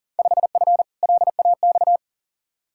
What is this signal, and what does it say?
Morse code, heard as a keyed tone, about 40 wpm, 700 Hz. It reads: 5F LAX
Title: Field Day Exchanges Encore - 40wpm